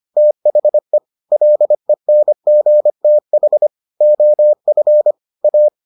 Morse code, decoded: THE LENGTH OF A

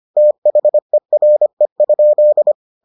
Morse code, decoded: THERE?

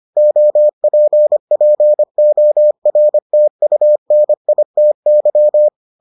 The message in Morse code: OPPORTUNITY

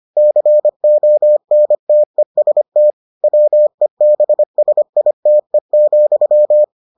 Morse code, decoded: CONTEST WEBSITE,

